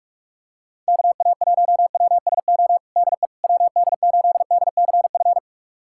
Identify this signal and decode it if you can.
Morse code, heard as a keyed tone, about 46 wpm, 700 Hz. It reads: KA1WSO DE WD8DCF
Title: USA Callsign DE Callsign - 6 Characters - 45wpm